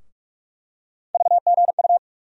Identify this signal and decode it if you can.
Morse code, heard as a keyed tone, about 44 wpm, 700 Hz. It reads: VGU
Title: USA Call Sign Postfixes - 3 Letters 45wpm